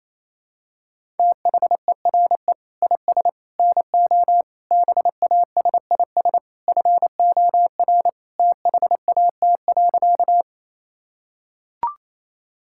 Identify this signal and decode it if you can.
Morse code, heard as a keyed tone, about 28 wpm, 700 Hz. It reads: THERE IS NO BASIS FOR THAT.
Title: Sentences from Top 1000 Words - Review Part 5 - 28wpm